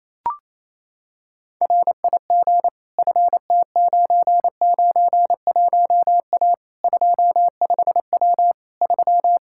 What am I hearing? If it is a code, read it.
Morse code, heard as a keyed tone, about 28 wpm, 700 Hz. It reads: RIG FT991A 25W 3
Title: K1USN SST QSO Practice 28wpm